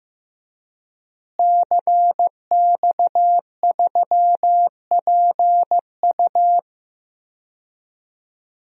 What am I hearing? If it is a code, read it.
Morse code, heard as a keyed tone, about 15 wpm, 700 Hz. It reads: CX3PU
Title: DXCC-100 Call Signs - 5 Characters - Repeated 2x - 15wpm